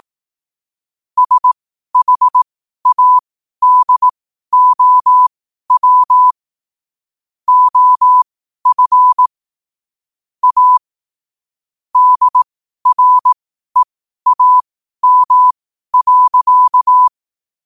Morse code, heard as a keyed tone, about 18 wpm, spaced as for 12 wpm, 1000 Hz. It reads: SHADOW OF A DREAM.
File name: qotdfiles / 12wpm